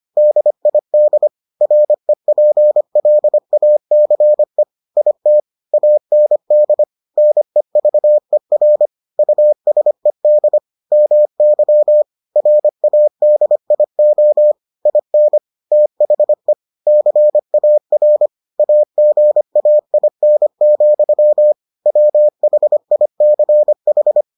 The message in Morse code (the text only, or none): DID REPLACE IT AND NEVER USED MY RADIO IN THE CAR AGAIN, WHICH